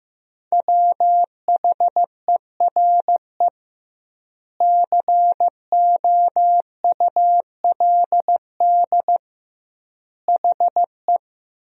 Morse code, heard as a keyed tone, about 15 wpm, 700 Hz. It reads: WHERE COULD HE
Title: 2x Word-Spacing - Sentences from Top 200 Words - 15wpm